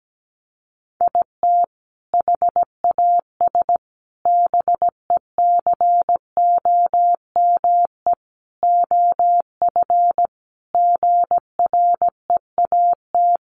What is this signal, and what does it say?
Morse code, heard as a keyed tone, about 17 wpm, 700 Hz. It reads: IT HAS BECOME OF GREAT
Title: Sentences from Top 1200 Words 17wpm